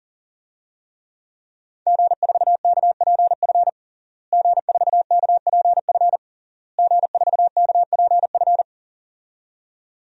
Morse code, heard as a keyed tone, about 40 wpm, 700 Hz. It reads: G4KPF G4KPF G4KPF
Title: DXCC-100 Call Signs - 5 Characters - Repeated 3x - 40wpm